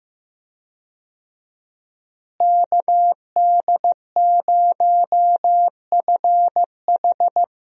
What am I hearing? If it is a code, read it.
Morse code, heard as a keyed tone, about 15 wpm, 700 Hz. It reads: KD0FH
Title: USA Call Signs - 5 Characters - Repeated 2x - 15wpm